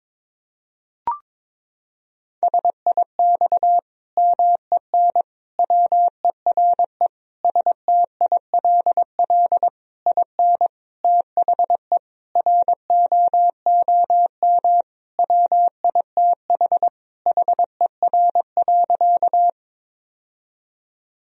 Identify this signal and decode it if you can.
Morse code, heard as a keyed tone, about 22 wpm, 700 Hz. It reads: SIX MEN WERE STILL IN THE ROOM WITH HER.
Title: Sentences from Top 1000 Words - Review Part 2 - 22wpm